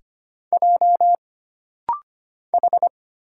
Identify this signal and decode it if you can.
Morse code, heard as a keyed tone, about 25 wpm, 700 Hz. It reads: J H
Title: Single Letter-Number - Rapid-Fire 25wpm